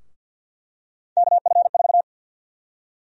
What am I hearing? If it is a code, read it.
Morse code, heard as a keyed tone, about 51 wpm, 700 Hz. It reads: KF4
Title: USA Call Sign Prefixes - Repeated 2x - 50wpm